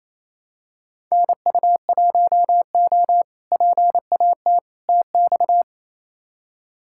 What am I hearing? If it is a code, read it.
Morse code, heard as a keyed tone, about 28 wpm, 700 Hz. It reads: NU1O PAT TX
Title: K1USN SST QSO Practice - 28wpm